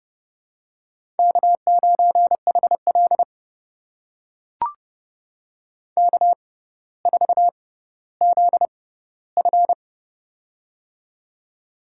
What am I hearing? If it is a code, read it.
Morse code, heard as a keyed tone, about 30 wpm, spaced as for 15 wpm, 700 Hz. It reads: K9HL K4ZF
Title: USA Call Signs - 4 Characters - 6x Character Spacing - 30wpm